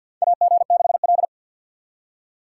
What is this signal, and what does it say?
Morse code, heard as a keyed tone, about 51 wpm, 700 Hz. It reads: AG6L
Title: USA Call Signs - 4 Characters - 2x Character Spacing - 50wpm